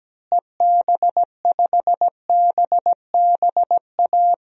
Morse code, heard as a keyed tone, about 17 wpm, 700 Hz. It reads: EB5BBA
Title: DXCC-100 Call Signs - 6 Characters - Repeated 3x - 17wpm